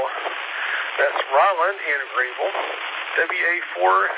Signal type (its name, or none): Emergency Alert System (SAME) tones